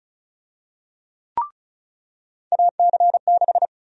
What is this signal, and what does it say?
Morse code, heard as a keyed tone, about 36 wpm, 700 Hz. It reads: AC6
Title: USA Call Sign Prefixes - 35wpm